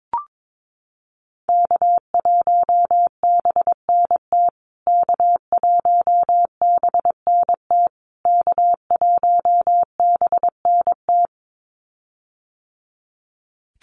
Morse code, heard as a keyed tone, about 22 wpm, 700 Hz. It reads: K1BNT K1BNT K1BNT
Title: USA Call Signs Repeated 3x 22wpm